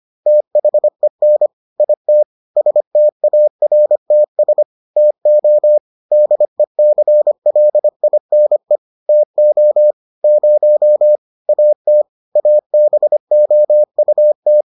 Morse code, heard as a keyed tone, about 25 wpm, 600 Hz. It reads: THEN IT STARTS TO DECLINE TO 0 AT ABOUT